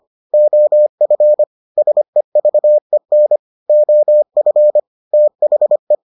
Morse code, heard as a keyed tone, about 25 wpm, 600 Hz. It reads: OF SEVEN OF THE